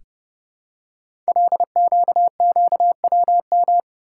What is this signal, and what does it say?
Morse code, heard as a keyed tone, about 30 wpm, 700 Hz. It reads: LQQWM
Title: Code Groups - 5 Letters - 30wpm